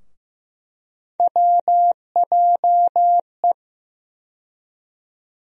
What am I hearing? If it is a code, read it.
Morse code, heard as a keyed tone, about 15 wpm, 700 Hz. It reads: WJE
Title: USA Call Sign Postfixes - 3 Letters 15wpm